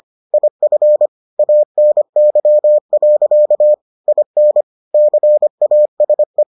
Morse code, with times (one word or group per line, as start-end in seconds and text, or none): 0.3-1.1 IF
1.4-3.7 ANY.
4.1-4.6 IN
4.9-6.4 CASE